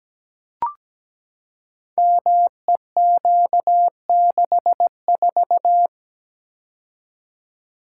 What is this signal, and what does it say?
Morse code, heard as a keyed tone, about 17 wpm, 700 Hz. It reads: MEQ64
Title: Code Groups - 5 Letter:Numbers - 17wpm